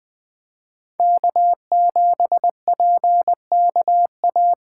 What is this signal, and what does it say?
Morse code, heard as a keyed tone, about 20 wpm, 700 Hz. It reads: K7PKA
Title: Speed Racing - USA 1x3 Call Signs 17wpm